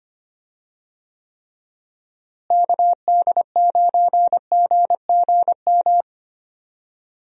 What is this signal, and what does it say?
Morse code, heard as a keyed tone, about 25 wpm, 700 Hz. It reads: KD9GGM
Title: USA Call Signs - 6 Characters - Repeated 3x - 25wpm